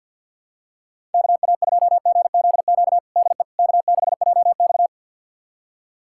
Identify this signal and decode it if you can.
Morse code, heard as a keyed tone, about 47 wpm, 700 Hz. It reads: KA2GZQ DE K6JX
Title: USA Callsign DE Callsign 50wpm